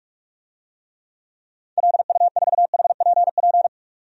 Morse code, heard as a keyed tone, about 46 wpm, 700 Hz. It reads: LU4HPP
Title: DXCC-100 Call Signs - 6 Characters - 45wpm